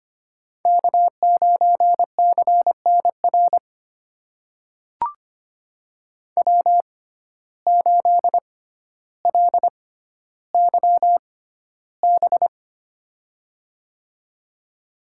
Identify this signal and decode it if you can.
Morse code, heard as a keyed tone, about 25 wpm, spaced as for 13 wpm, 700 Hz. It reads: K9CNR W8LYB
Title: USA Call Signs - 5 Characters - 6x Character Spacing - 25wpm